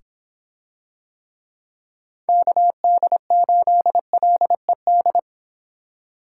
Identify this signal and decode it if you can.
Morse code, heard as a keyed tone, about 26 wpm, 700 Hz. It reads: KD8LED